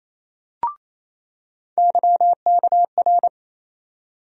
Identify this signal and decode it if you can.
Morse code, heard as a keyed tone, about 28 wpm, 700 Hz. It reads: YKR
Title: USA Call Sign Postfixes - 3 Letters 28wpm